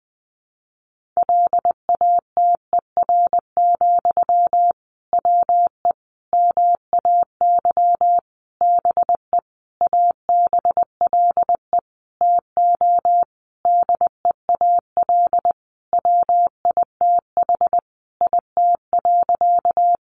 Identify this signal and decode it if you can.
Morse code, heard as a keyed tone, about 20 wpm, 700 Hz. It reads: LATER, WE MAY BE ABLE TO DEAL WITH IT.
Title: Sentences from Top 800 Words 20wpm